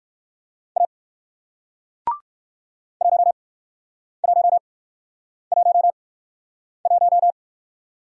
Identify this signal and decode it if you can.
Morse code, heard as a keyed tone, about 58 wpm, spaced as for 9 wpm, 700 Hz. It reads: A 1111